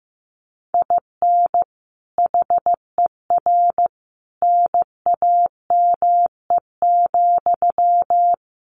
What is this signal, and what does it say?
Morse code, heard as a keyed tone, about 15 wpm, 700 Hz. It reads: IN HER NAME,
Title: Sentences from Top 500 Words 15wpm